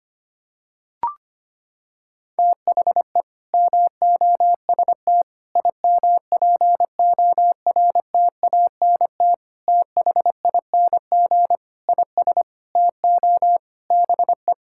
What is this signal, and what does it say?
Morse code, heard as a keyed tone, about 25 wpm, 700 Hz. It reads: THE MOST IMPORTANT THING IS TO BE
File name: practice / 25wpm / Sentences with 4-Word Phrases - 25wpm